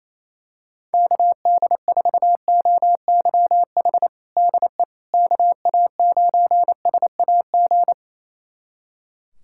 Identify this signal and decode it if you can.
Morse code, heard as a keyed tone, about 27 wpm, 700 Hz. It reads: KD4OYH DE KA9SAG
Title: USA Callsign DE Callsign - 6 Characters - 28wpm